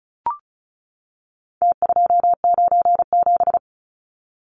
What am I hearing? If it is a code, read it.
Morse code, heard as a keyed tone, about 34 wpm, 700 Hz. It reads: T297
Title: Numbers - Contest Style - 4-Digit 35wpm